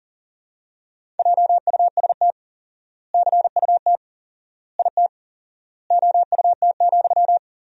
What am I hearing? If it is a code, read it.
Morse code, heard as a keyed tone, about 40 wpm, 700 Hz. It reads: JUST CUT IT OUT,
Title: 4x Word-Spacing - Sentences from Top 300 Words - 40wpm